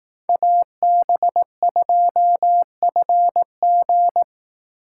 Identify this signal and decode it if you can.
Morse code, heard as a keyed tone, about 18 wpm, 700 Hz. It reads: AB2FG
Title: Speed Racing - USA 2x2 Call Signs 15wpm